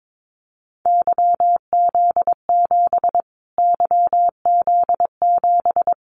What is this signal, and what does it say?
Morse code, heard as a keyed tone, about 22 wpm, 700 Hz. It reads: YZ7 YZ7
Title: Callsign Prefixes for Top 25 DXCC - Repeated 3x - 22wpm